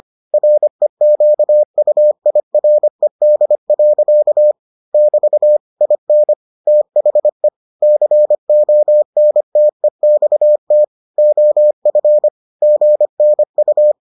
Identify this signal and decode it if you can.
Morse code, heard as a keyed tone, about 25 wpm, 600 Hz. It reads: REQUIRED. <BT> IN THE CONTEXT OF GNU